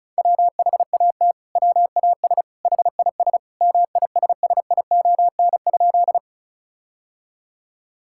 Morse code, heard as a keyed tone, about 36 wpm, 700 Hz. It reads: WHAT WAS HIS MISSION?
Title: Sentences from Top 1600 Words 35wpm